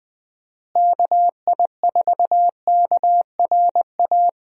Morse code, heard as a keyed tone, about 20 wpm, 700 Hz. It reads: KI4KRA
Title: USA Call Signs - 6 Characters - Repeated 3x - 20wpm